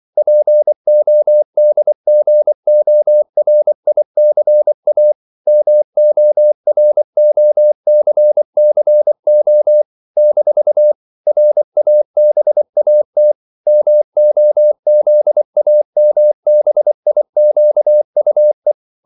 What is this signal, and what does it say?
Morse code, heard as a keyed tone, about 24 wpm, 600 Hz. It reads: PODGORICA MOROCCO - RABAT MOZAMBIQUE
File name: CW / Practice / NationalCapitals24